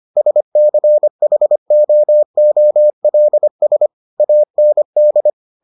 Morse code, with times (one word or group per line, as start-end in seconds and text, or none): 0.2-3.9 SCHOOLS
4.2-5.3 AND